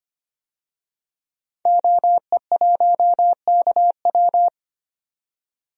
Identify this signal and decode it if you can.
Morse code, heard as a keyed tone, about 25 wpm, 700 Hz. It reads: OE1KW
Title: DXCC-100 Call Signs - 5 Characters - 25wpm